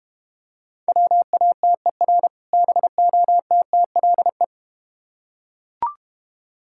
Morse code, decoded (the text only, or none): WATER BOTTLE